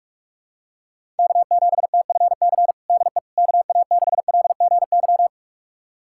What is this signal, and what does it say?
Morse code, heard as a keyed tone, about 46 wpm, 700 Hz. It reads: K7TFC DE KA6LGY
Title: USA Callsign DE Callsign 45wpm